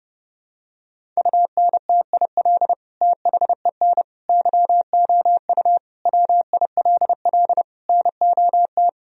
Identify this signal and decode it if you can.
Morse code, heard as a keyed tone, about 30 wpm, 700 Hz. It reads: UNTIL THEN YOU WILL NOT
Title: Sentences from Top 1000 Words - Review Part 1 - 30wpm